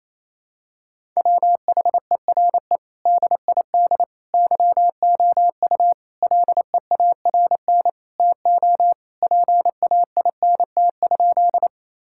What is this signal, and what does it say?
Morse code, heard as a keyed tone, about 28 wpm, 700 Hz. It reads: WHERE DID YOU LEARN TO PAINT?
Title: Sentences from Top 1000 Words - Review Part 6 - 28wpm